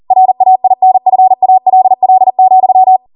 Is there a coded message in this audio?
Yes: Morse code from 0.1 to 3.0 s, heard as a keyed tone, about 40 wpm, 750 Hz. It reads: RAINFALL,